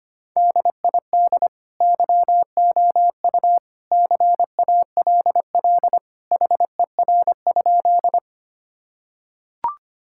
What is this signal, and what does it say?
Morse code, heard as a keyed tone, about 25 wpm, 700 Hz. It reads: DID YOU CALL HER?
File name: practice / 25wpm / Sentences with 2-Word Phrases - 25wpm